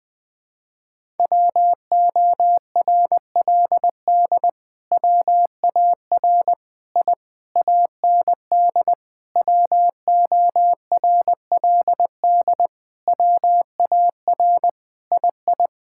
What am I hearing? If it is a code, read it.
Morse code, heard as a keyed tone, about 20 wpm, 700 Hz. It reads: WORLD WAR I AND WORLD WAR II
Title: Sentences with 7-Word Phrases - Spoken Beginning - 20wpm